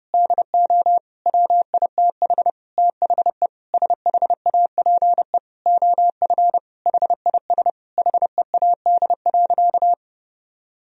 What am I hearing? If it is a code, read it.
Morse code, heard as a keyed tone, about 30 wpm, 700 Hz. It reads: DO WITH THE SHAPE OF HIS HEAD.
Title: Sentences from Top 1000 Words - Review Part 3 - 30wpm